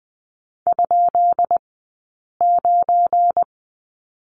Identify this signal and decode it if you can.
Morse code, heard as a keyed tone, about 20 wpm, 700 Hz. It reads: ? 9
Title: 190 Lesson 31d - Letter v - Two Characters 20wpm